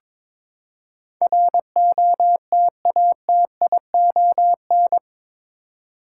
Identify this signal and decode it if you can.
Morse code, heard as a keyed tone, about 22 wpm, 700 Hz. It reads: ROTATION
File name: practice / 22wpm / Eight Letter Words - 22wpm